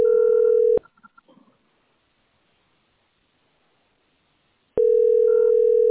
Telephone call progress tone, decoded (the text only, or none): ringback tone